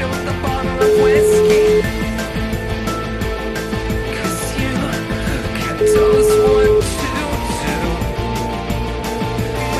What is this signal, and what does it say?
Ringback tone (Europe, 425 Hz), with speech.